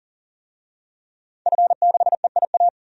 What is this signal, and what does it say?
Morse code, heard as a keyed tone, about 40 wpm, 700 Hz. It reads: F6EIA
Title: DXCC-100 Call Signs - 5 Characters - 40wpm